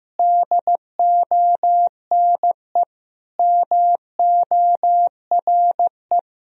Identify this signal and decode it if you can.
Morse code, heard as a keyed tone, about 15 wpm, 700 Hz. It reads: DONE MORE
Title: Sentences from Top 1000 Words - Review Part 2 - 15wpm